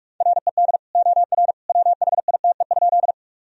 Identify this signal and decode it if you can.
Morse code, heard as a keyed tone, about 41 wpm, 700 Hz. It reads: RED OR WHITE?
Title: Sentences from Top 1000 Words - Review Part 2 - 45wpm